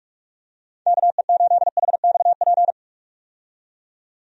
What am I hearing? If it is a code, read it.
Morse code, heard as a keyed tone, about 46 wpm, 700 Hz. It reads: KE8HXP
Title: USA Call Signs - 6 Characters - 45wpm